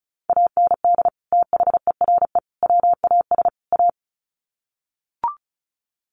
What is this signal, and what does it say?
Morse code, heard as a keyed tone, about 34 wpm, 700 Hz. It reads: AND THERE WAS A
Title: Sets of 4 Words 35wpm